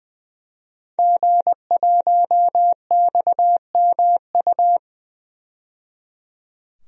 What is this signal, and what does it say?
Morse code, heard as a keyed tone, about 21 wpm, 700 Hz. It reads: G1XMU
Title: DXCC-100 Call Signs - 5 Characters - 20wpm